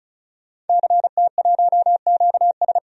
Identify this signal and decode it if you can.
Morse code, heard as a keyed tone, about 34 wpm, 700 Hz. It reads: CT1QS
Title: DXCC-100 Call Signs - 5 Characters - 35wpm